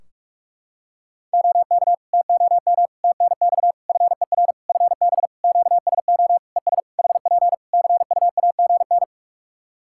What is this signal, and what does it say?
Morse code, heard as a keyed tone, about 48 wpm, 700 Hz. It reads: OK TOM TNX FER FB QSO ES HP CUAGN
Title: Practice QSO - Exchange 4 - Ending - 45wpm